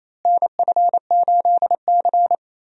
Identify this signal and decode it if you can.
Morse code, heard as a keyed tone, about 28 wpm, 700 Hz. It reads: NF8C
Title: USA Call Signs - 4 Characters - Repeated 3x - 28wpm